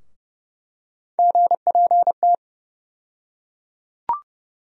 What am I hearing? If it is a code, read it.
Morse code, heard as a keyed tone, about 30 wpm, 700 Hz. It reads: GPT
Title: USA Call Sign Postfixes - 3 Letters 30wpm